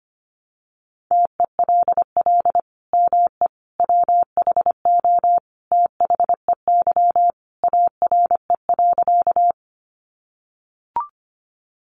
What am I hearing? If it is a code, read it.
Morse code, heard as a keyed tone, about 25 wpm, 700 Hz. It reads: TELL ME WHO THEY ARE.
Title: Sentences from Top 700 Words 25wpm